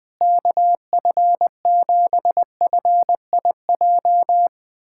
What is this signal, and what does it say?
Morse code, heard as a keyed tone, about 20 wpm, 700 Hz. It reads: KF7FIJ
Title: USA Callsign DE Callsign - 6 Characters - 20wpm